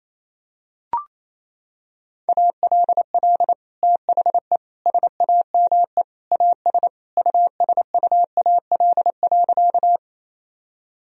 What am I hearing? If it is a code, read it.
Morse code, heard as a keyed tone, about 28 wpm, 700 Hz. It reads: ALL THE SAME AS USUAL.
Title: Sentences from Top 1000 Words - Review Part 2 - 28wpm